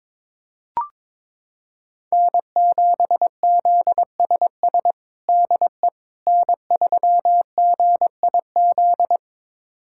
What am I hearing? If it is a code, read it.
Morse code, heard as a keyed tone, about 22 wpm, 700 Hz. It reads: N7ZSS DE N3GIZ
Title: USA Callsign DE Callsign - 5 Characters - 22wpm